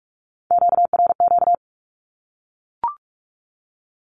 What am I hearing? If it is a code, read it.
Morse code, heard as a keyed tone, about 44 wpm, 700 Hz. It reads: QRQ
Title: Q-Codes 45wpm